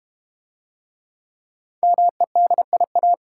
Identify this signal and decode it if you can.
Morse code, heard as a keyed tone, about 32 wpm, 700 Hz. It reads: MEDIA